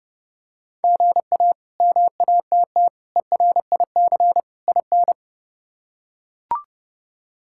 Morse code, heard as a keyed tone, about 32 wpm, 700 Hz. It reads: GA MATT ERIC IN
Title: K1USN SST QSO Practice - 30wpm